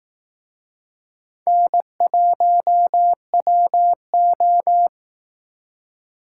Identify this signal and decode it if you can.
Morse code, heard as a keyed tone, about 19 wpm, 700 Hz. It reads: N1WO
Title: Speed Racing - USA 1x2 Call Signs 15wpm